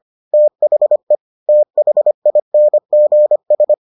Morse code, heard as a keyed tone, about 25 wpm, 600 Hz. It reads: THE THINGS